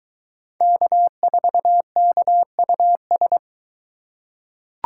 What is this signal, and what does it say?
Morse code, heard as a keyed tone, about 22 wpm, 700 Hz. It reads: K4KUS